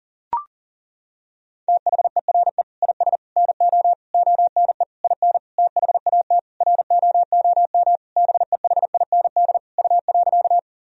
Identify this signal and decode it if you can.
Morse code, heard as a keyed tone, about 40 wpm, 700 Hz. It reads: THERE IS NO ONE IN THAT ROOM BEHIND U.
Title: Sentences from Top 500 Words and Abbreviations 40wpm